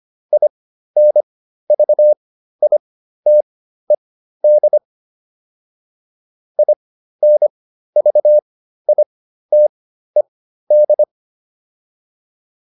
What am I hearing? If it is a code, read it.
Morse code, heard as a keyed tone, about 25 wpm, spaced as for 13 wpm, 600 Hz. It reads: INVITED INVITED